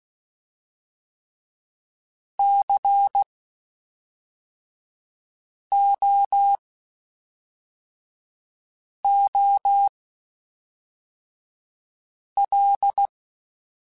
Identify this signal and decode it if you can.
Morse code, heard as a keyed tone, about 16 wpm, spaced as for 3 wpm, 800 Hz. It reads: COOL
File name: Spaced Characters / Spaced Characters-mp3s / Weather5 1 sp.@5